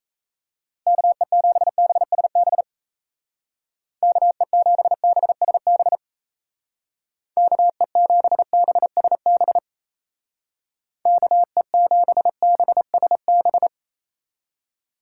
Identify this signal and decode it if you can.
Morse code, heard as a keyed tone, about 35 wpm, 700 Hz. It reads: KE7BSB KE7BSB KE7BSB KE7BSB
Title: Speed Racing - USA 2x3 Call Signs 28wpm